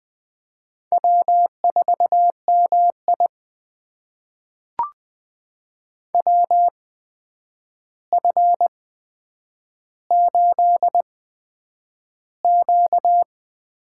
Morse code, heard as a keyed tone, about 20 wpm, spaced as for 9 wpm, 700 Hz. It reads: W4MI WF8Q
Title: USA Call Signs - 8x Character Spacing - 20wpm